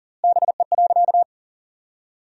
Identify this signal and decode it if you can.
Morse code, heard as a keyed tone, about 40 wpm, 700 Hz. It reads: BE.
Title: 4x Word-Spacing - Sentences from Top 300 Words - 40wpm